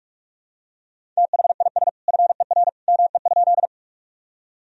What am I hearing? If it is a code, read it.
Morse code, heard as a keyed tone, about 41 wpm, 700 Hz. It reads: THIS FER ME?
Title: Sentences from Top 200 Words and Abbreviations 45wpm